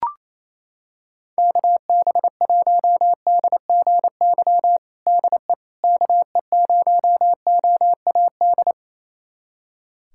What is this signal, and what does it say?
Morse code, heard as a keyed tone, about 27 wpm, 700 Hz. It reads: KB1DGY DE KE0OAD
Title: USA Callsign DE Callsign - 6 Characters - 28wpm